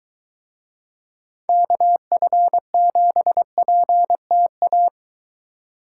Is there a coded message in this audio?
Yes: Morse code from 1.5 to 4.9 s, heard as a keyed tone, about 23 wpm, 700 Hz. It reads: KF7PTA